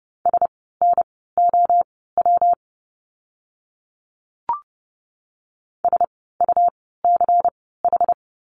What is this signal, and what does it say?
Morse code, heard as a keyed tone, about 29 wpm, spaced as for 17 wpm, 700 Hz. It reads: SNOW SUCH
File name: course / 30:10wpm / 115 Lesson 19f - Letter w - Words 30wpm